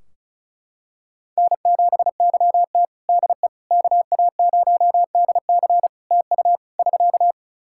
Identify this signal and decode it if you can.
Morse code, heard as a keyed tone, about 34 wpm, 700 Hz. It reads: N7YT DE KA0DC TU <SK>
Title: Practice QSO - Exchange 4 - Ending - 35wpm